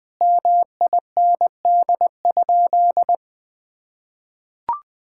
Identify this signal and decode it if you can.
Morse code, heard as a keyed tone, about 20 wpm, 700 Hz. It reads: MIND?
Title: Sentences from Top 1000 Words - Review Part 6 - 20wpm